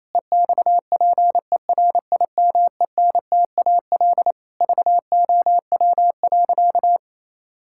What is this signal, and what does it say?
Morse code, heard as a keyed tone, about 28 wpm, 700 Hz. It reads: EXPERIMENTAL VOW.
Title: Pangram Challenge v2 - 28wpm